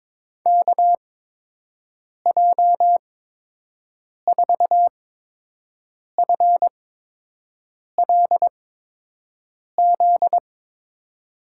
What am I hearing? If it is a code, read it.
Morse code, heard as a keyed tone, about 21 wpm, spaced as for 6 wpm, 700 Hz. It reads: KJ4FLZ
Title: USA Call Signs - 6 Characters - 8x Character Spacing - 22wpm